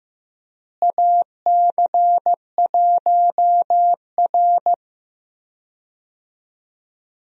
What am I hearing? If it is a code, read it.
Morse code, heard as a keyed tone, about 15 wpm, 700 Hz. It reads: AC1R
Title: USA Call Signs - 4 Characters - Repeated 3x - 15wpm